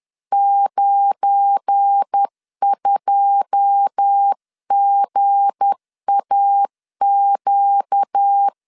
Morse code, heard as a keyed tone, about 10 wpm, 800 Hz. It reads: 92GAQ
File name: LEZIONI / 9 286 10wpm 2735